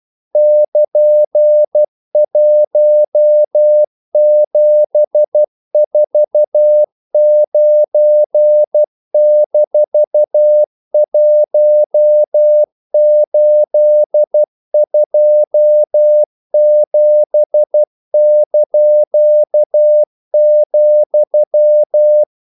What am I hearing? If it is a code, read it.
Morse code, heard as a keyed tone, about 12 wpm, 600 Hz. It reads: <KN>1749-1827),
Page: English quotations in CW (telegraphy - morse)